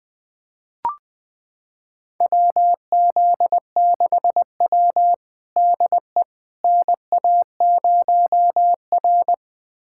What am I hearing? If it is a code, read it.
Morse code, heard as a keyed tone, about 21 wpm, 700 Hz. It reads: WZ6W DE NA0R
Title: USA Callsign DE Callsign - 4 Characters - 20wpm